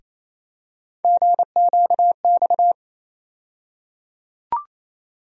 Morse code, heard as a keyed tone, about 28 wpm, 700 Hz. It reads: GQX
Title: USA Call Sign Postfixes - 3 Letters 28wpm